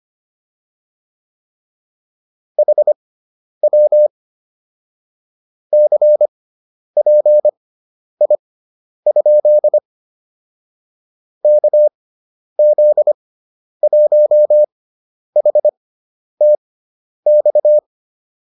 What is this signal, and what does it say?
Morse code, heard as a keyed tone, about 25 wpm, spaced as for 10 wpm, 600 Hz. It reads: HW CPI? KZ1HTX
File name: QSO301 10